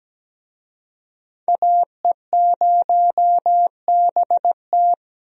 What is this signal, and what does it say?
Morse code, heard as a keyed tone, about 17 wpm, 700 Hz. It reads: AE0BT
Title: USA Call Signs - 5 Characters - Repeated 2x - 17wpm